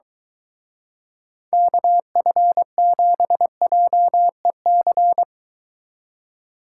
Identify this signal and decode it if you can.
Morse code, heard as a keyed tone, about 23 wpm, 700 Hz. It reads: KF7JEC